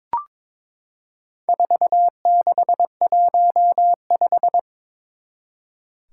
Morse code, heard as a keyed tone, about 22 wpm, 700 Hz. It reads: 4615
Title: Four Challenging Letter-Number Combinations 22wpm